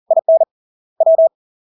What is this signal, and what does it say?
Morse code, heard as a keyed tone, about 40 wpm, 650 Hz. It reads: IN W
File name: LARK-FD-2022-Log-in-CW-40wpm-650-Hz